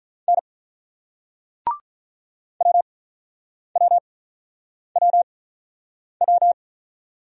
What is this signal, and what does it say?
Morse code, heard as a keyed tone, about 50 wpm, spaced as for 9 wpm, 700 Hz. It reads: N WWWW